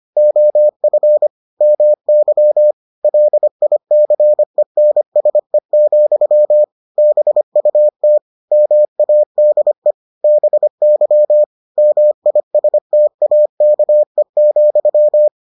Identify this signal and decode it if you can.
Morse code, heard as a keyed tone, about 25 wpm, 600 Hz. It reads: OF MY LICENSE, BUT MADE BY MISTAKE,